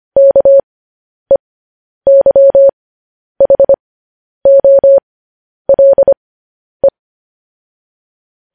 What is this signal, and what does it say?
Morse code, heard as a keyed tone, about 25 wpm, spaced as for 10 wpm, 550 Hz. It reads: KEYHOLE
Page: News Headlines in Morse Code at 10 WPM